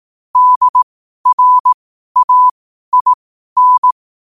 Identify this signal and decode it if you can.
Morse code, heard as a keyed tone, about 18 wpm, spaced as for 12 wpm, 1000 Hz. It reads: DRAIN
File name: qotdfiles / 12wpm